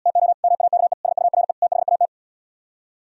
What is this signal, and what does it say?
Morse code, heard as a keyed tone, about 77 wpm, 700 Hz. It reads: TO MAKE SURE THAT